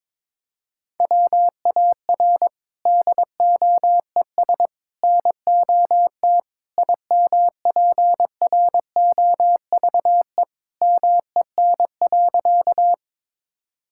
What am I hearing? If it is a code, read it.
Morse code, heard as a keyed tone, about 22 wpm, 700 Hz. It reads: WAR DOES NOT IMPROVE MEN.
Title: Sentences from Top 1000 Words - Review Part 5 - 22wpm